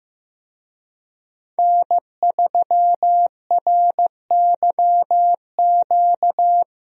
Morse code, heard as a keyed tone, about 15 wpm, 700 Hz. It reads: N3RYQ
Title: USA Call Signs - 5 Characters - Repeated 2x - 15wpm